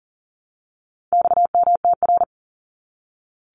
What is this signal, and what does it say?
Morse code, heard as a keyed tone, about 40 wpm, 700 Hz. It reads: XMTR
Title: Abbreviations Repeated 3x - 40wpm